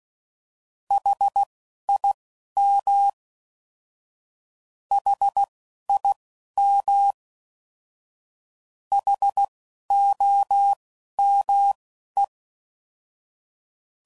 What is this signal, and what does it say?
Morse code, heard as a keyed tone, about 16 wpm, spaced as for 11 wpm, 800 Hz. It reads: HIM HIM HOME